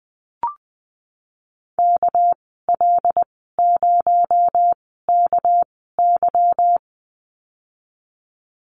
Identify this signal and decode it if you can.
Morse code, heard as a keyed tone, about 19 wpm, spaced as for 14 wpm, 700 Hz. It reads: KL0KY
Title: 238 Lesson 38g - Letter q - Call Signs 20wpm